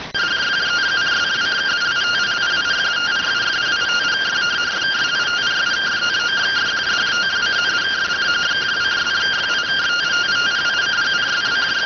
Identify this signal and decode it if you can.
Two-tone teleprinter signal, about 170 Hz shift.